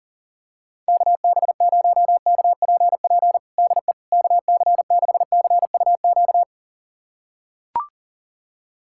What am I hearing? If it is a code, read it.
Morse code, heard as a keyed tone, about 40 wpm, 700 Hz. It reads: KB0KPP DE KC6CUQ
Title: USA Callsign DE Callsign 40wpm